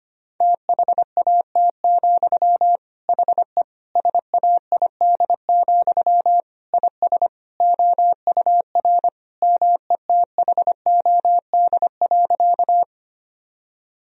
Morse code, heard as a keyed tone, about 25 wpm, 700 Hz. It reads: THAT, HE SAID, IS OUR METHOD.
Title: Sentences from Top 1000 Words - Review Part 3 - 25wpm